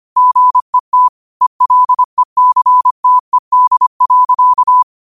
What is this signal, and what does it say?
Morse code, heard as a keyed tone, about 25 wpm, 1000 Hz. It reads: GET ELECTED.
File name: qotdfiles / 25wpm